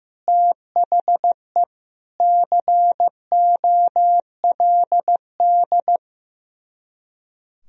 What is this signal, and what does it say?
Morse code, heard as a keyed tone, about 15 wpm, 700 Hz. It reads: THE COLD
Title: Speed-Racing - Sets of 5 Words 15wpm